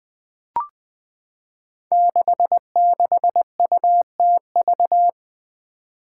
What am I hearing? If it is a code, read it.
Morse code, heard as a keyed tone, about 21 wpm, 700 Hz. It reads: 66UTV
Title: Five Challenging Letter-Number Combinations 20wpm